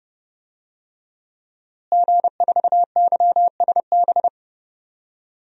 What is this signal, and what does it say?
Morse code, heard as a keyed tone, about 31 wpm, 700 Hz. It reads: G4YSB
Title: DXCC-100 Call Signs - 5 Characters - Repeated 2x - 30wpm